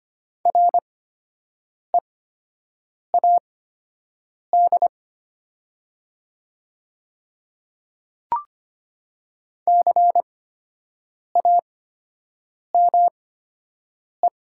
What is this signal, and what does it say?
Morse code, heard as a keyed tone, about 25 wpm, spaced as for 7 wpm, 700 Hz. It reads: READ CAME
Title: 8x Character Spacing - Top 200 Words - 25wpm